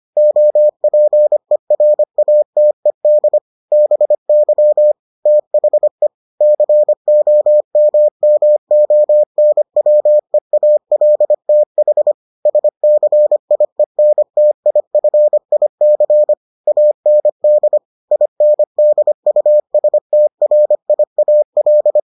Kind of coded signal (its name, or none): Morse code